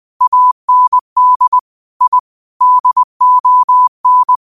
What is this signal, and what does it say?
Morse code, heard as a keyed tone, about 20 wpm, 1000 Hz. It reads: AND I DON